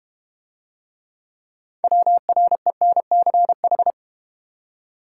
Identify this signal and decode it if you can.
Morse code, heard as a keyed tone, about 31 wpm, 700 Hz. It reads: WRENCH
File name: Easy Words - Speed-Racing with Hints - 25wpm